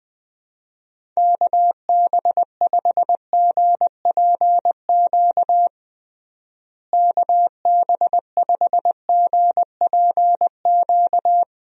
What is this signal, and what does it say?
Morse code, heard as a keyed tone, about 21 wpm, 700 Hz. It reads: KB5GPQ KB5GPQ
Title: USA Call Signs - 6 Characters - Repeated 3x - 20wpm